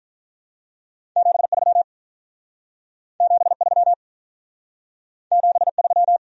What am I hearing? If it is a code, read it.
Morse code, heard as a keyed tone, about 46 wpm, 700 Hz. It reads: 73 73 73